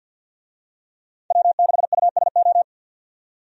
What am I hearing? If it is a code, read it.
Morse code, heard as a keyed tone, about 50 wpm, 700 Hz. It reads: W6USO
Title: USA Call Signs - 5 Characters - 50wpm